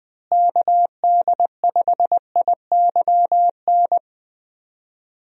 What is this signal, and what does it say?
Morse code, heard as a keyed tone, about 21 wpm, 700 Hz. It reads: KD5IYN
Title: USA Call Signs - 6 Characters - Repeated 3x - 20wpm